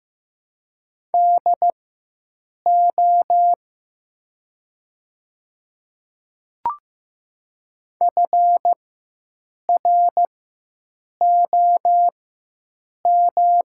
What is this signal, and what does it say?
Morse code, heard as a keyed tone, about 15 wpm, spaced as for 7 wpm, 700 Hz. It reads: DO FROM
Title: 4x Character Spacing - Top 300 Words - 15wpm